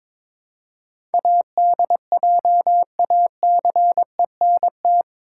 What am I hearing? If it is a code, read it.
Morse code, heard as a keyed tone, about 22 wpm, 700 Hz. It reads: ADJACENT
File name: practice / 22wpm / Eight Letter Words - 22wpm